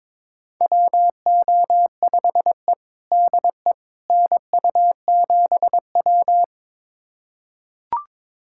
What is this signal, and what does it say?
Morse code, heard as a keyed tone, about 22 wpm, 700 Hz. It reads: WO5E DE NU7W
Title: USA Callsign DE Callsign - 4 Characters - 22wpm